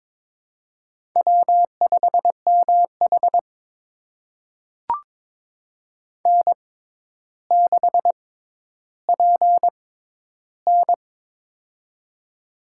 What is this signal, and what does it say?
Morse code, heard as a keyed tone, about 22 wpm, spaced as for 11 wpm, 700 Hz. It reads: W5MH N6PN
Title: USA Call Signs - 4 Characters - 6x Character Spacing - 22wpm